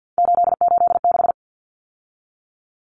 Morse code, heard as a keyed tone, about 50 wpm, 700 Hz. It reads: 886
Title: Numbers - 3-Digit 50wpm